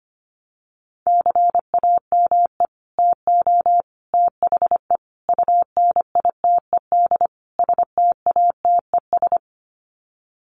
Morse code, heard as a keyed tone, about 25 wpm, 700 Hz. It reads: CAME TO THE UNITED STATES
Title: Sets of 5 Words Encore2 25wpm